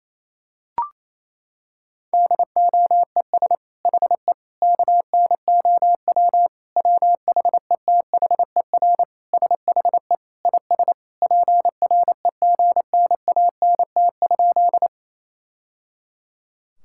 Morse code, heard as a keyed tone, about 28 wpm, 700 Hz. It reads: DOES HE KNOW WHETHER SHE IS PREGNANT?
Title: Sentences from Top 1000 Words - Review Part 6 - 28wpm